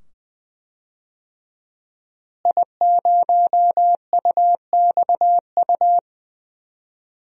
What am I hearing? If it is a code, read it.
Morse code, heard as a keyed tone, about 21 wpm, 700 Hz. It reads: I0UXU
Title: DXCC-100 Call Signs - 5 Characters - Repeated 2x - 20wpm